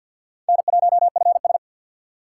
Morse code, heard as a keyed tone, about 62 wpm, 700 Hz. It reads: N1FS
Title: Speed Racing - USA 1x2 Call Signs 50wpm